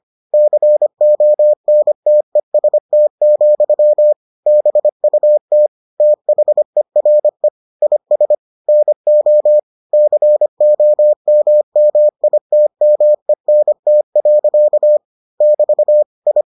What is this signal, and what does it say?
Morse code, heard as a keyed tone, about 25 wpm, 600 Hz. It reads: CONTEST, BUT THERE IS NO COMMITMENT. <BT> I